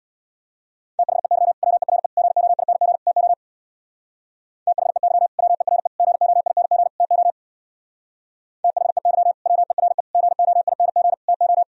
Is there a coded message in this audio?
Yes: Morse code from 1.0 to 11.6 s, heard as a keyed tone, about 64 wpm, 700 Hz. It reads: THEY WERE GOING TO THEY WERE GOING TO THEY WERE GOING TO